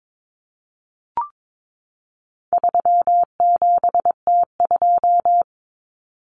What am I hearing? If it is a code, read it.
Morse code, heard as a keyed tone, about 22 wpm, 700 Hz. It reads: 37T2
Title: Common Cut Numbers - 4-Digit 22wpm